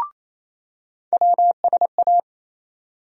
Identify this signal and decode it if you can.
Morse code, heard as a keyed tone, about 28 wpm, 700 Hz. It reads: WSA
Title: USA Call Sign Postfixes - 3 Letters 28wpm